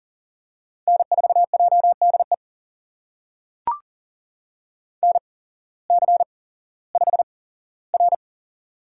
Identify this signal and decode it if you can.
Morse code, heard as a keyed tone, about 40 wpm, spaced as for 19 wpm, 700 Hz. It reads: N4JDE NC5R
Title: USA Call Signs - 8x Character Spacing - 40wpm